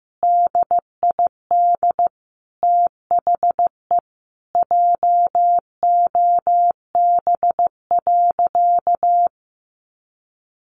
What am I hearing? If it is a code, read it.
Morse code, heard as a keyed tone, about 15 wpm, 700 Hz. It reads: DID THE JOB.
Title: Sentences from Top 700 Words 15wpm